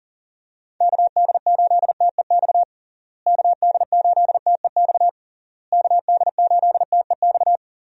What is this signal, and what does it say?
Morse code, heard as a keyed tone, about 40 wpm, 700 Hz. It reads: KD8TEX KD8TEX KD8TEX
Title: USA Call Signs - 6 Characters - Repeated 3x - 40wpm